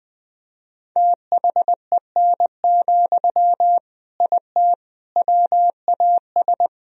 Morse code, heard as a keyed tone, about 20 wpm, 700 Hz. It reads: THEN, IT WAS
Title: Sentences from Top 1000 Words - Review Part 1 - 20wpm